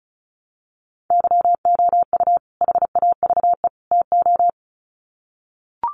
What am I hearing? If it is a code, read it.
Morse code, heard as a keyed tone, about 34 wpm, 700 Hz. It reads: YOU HAVE TO
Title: Sets of 3 Words 35wpm